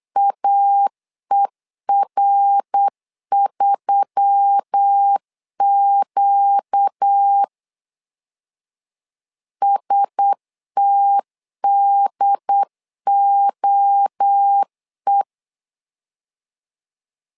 Morse code, heard as a keyed tone, about 8 wpm, 800 Hz. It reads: AER3Q STDOE
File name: LEZIONI / 4 Q1S3 8wpm 3037